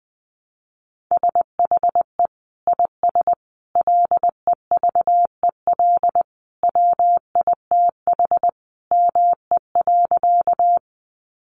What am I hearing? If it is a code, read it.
Morse code, heard as a keyed tone, about 20 wpm, 700 Hz. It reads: SHE IS LEVEL WITH ME.
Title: Sentences from Top 700 Words 20wpm